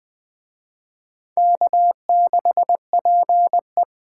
Morse code, heard as a keyed tone, about 20 wpm, 700 Hz. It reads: K6PE